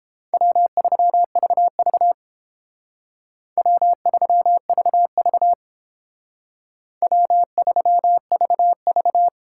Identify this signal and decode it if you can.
Morse code, heard as a keyed tone, about 28 wpm, 700 Hz. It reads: W3VV W3VV W3VV